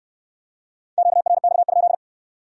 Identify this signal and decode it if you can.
Morse code, heard as a keyed tone, about 62 wpm, 700 Hz. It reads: QRX?